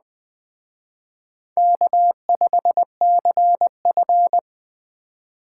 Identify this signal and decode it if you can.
Morse code, heard as a keyed tone, about 21 wpm, 700 Hz. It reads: K5CF
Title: Speed Racing - USA 1x2 Call Signs 17wpm